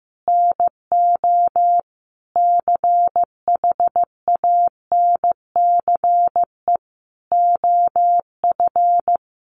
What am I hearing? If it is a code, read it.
Morse code, heard as a keyed tone, about 15 wpm, 700 Hz. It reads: NO CHANCE OF
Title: Sentences from Top 800 Words 15wpm